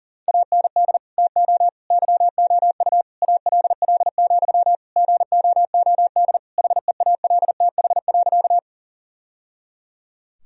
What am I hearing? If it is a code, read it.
Morse code, heard as a keyed tone, about 37 wpm, 700 Hz. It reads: AND TO YOU ALL, GOOD HEALTH.
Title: Sentences from Top 1000 Words - Review Part 3 - 40wpm